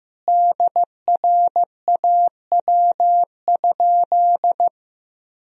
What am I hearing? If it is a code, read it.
Morse code, heard as a keyed tone, about 16 wpm, 700 Hz. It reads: DRAW?
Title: Sentences from Top 1000 Words - Review Part 1 - 15wpm